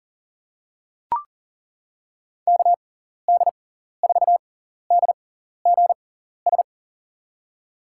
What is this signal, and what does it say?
Morse code, heard as a keyed tone, about 40 wpm, spaced as for 14 wpm, 700 Hz. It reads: KD4DGS
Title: USA Call Signs - 6x Character Spacing - 40wpm